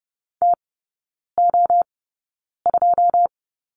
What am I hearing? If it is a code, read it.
Morse code, heard as a keyed tone, about 30 wpm, spaced as for 9 wpm, 690 Hz. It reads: TO2
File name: Lesson 16e - Letter c - Three Characters 30wpm